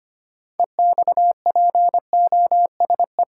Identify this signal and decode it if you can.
Morse code, heard as a keyed tone, about 25 wpm, 700 Hz. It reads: EXPOSE
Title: Six Letter Words - 25wpm